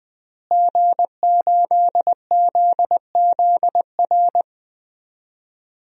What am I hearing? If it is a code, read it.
Morse code, heard as a keyed tone, about 20 wpm, 700 Hz. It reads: G8ZZR
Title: DXCC-100 Call Signs - 5 Characters - Repeated 3x - 20wpm